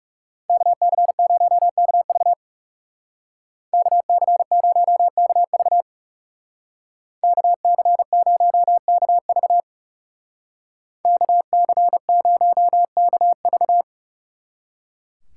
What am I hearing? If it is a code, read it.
Morse code, heard as a keyed tone, about 36 wpm, 700 Hz. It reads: KC0KV KC0KV KC0KV KC0KV
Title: Speed Racing - USA 2x2 Call Signs 30wpm